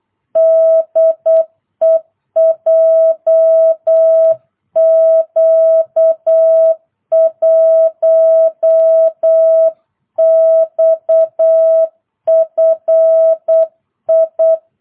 Morse code, heard as a keyed tone, about 8 wpm, 650 Hz. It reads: DE JQ1XFI